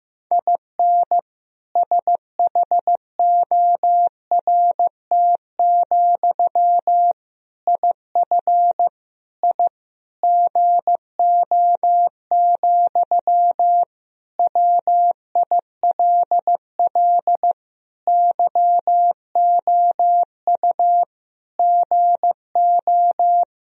Morse code, heard as a keyed tone, about 15 wpm, 700 Hz. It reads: IN SHORT, IF I GO, WILL YOU GO
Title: Sentences from Top 1000 Words - Review Part 2 - 15wpm